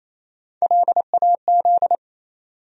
Morse code, heard as a keyed tone, about 27 wpm, 700 Hz. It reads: LAZ
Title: USA Call Sign Postfixes - 3 Letters 28wpm